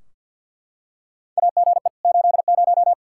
Morse code, heard as a keyed tone, about 52 wpm, 700 Hz. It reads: AGE 80
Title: Practice QSO - Exchange 3 - More Info - 50wpm